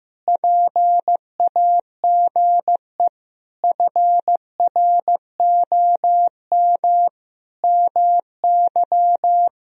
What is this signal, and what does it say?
Morse code, heard as a keyed tone, about 15 wpm, 700 Hz. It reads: PAGE FROM MY
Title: Sentences from Top 1000 Words - Review Part 1 - 15wpm